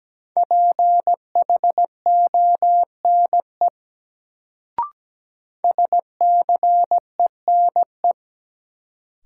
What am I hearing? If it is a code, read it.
Morse code, heard as a keyed tone, about 17 wpm, 700 Hz. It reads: PHONE SCENE
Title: Sentences from Top 1100 Words - New Word Drill - 17wpm